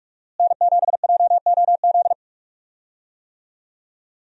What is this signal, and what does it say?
Morse code, heard as a keyed tone, about 42 wpm, 700 Hz. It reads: N7JOZ
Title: USA Call Signs - 5 Characters - Repeated 2x - 45wpm